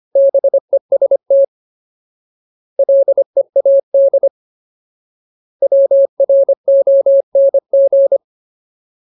Morse code, heard as a keyed tone, about 25 wpm, 550 Hz. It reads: BEST LEAD WRONG
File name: Ultimate Word List / 25WPM Ultimate